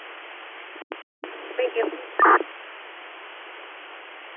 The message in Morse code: none

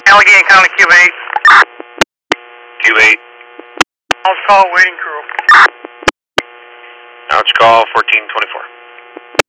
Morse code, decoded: none

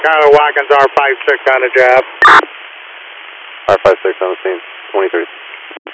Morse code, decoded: none

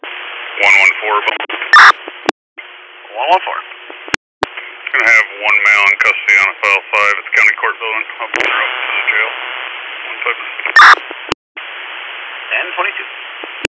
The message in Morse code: none

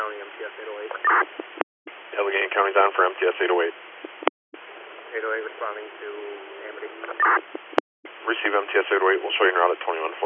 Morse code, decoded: none

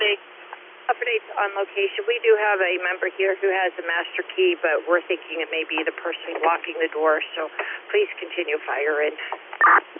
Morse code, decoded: none

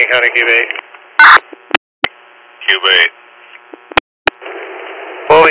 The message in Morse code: none

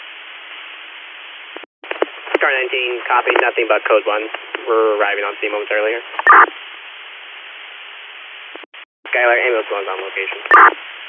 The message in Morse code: none

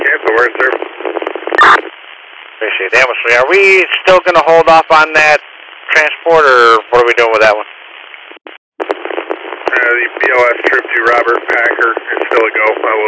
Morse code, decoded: none

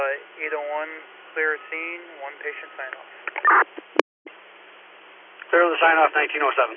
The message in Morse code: none

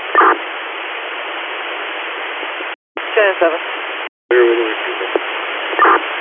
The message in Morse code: none

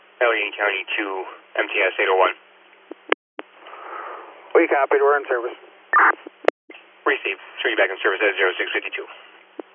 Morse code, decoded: none

